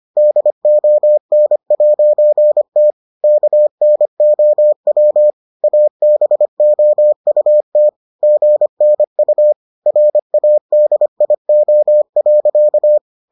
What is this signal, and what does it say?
Morse code, heard as a keyed tone, about 25 wpm, 600 Hz. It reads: DON'T KNOW ABOUT GNU RADIO.